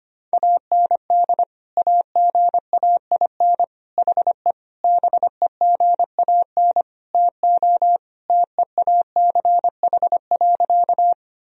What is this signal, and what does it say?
Morse code, heard as a keyed tone, about 25 wpm, 700 Hz. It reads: AND AGAIN HE BEGAN TO TEACH.
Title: Sentences from Top 1000 Words - Review Part 2 - 25wpm